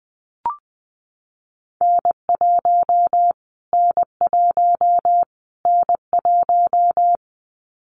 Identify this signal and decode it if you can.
Morse code, heard as a keyed tone, about 20 wpm, 700 Hz. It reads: N1 N1 N1
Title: Callsign Prefixes for Top 25 DXCC - Repeated 3x - 20wpm